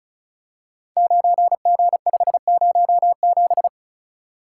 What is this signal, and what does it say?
Morse code, heard as a keyed tone, about 36 wpm, 700 Hz. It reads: 9G507
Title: Code Groups - 5 Letter:Numbers - 35wpm